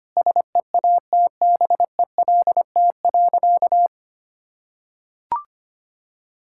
Morse code, heard as a keyed tone, about 25 wpm, 700 Hz. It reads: SEATBELT.
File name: Easy Sentences with Hints - 25wpm